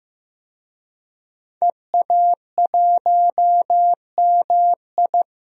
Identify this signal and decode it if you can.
Morse code, heard as a keyed tone, about 15 wpm, 700 Hz. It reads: EA1MI
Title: DXCC-100 Call Signs - 5 Characters - 15wpm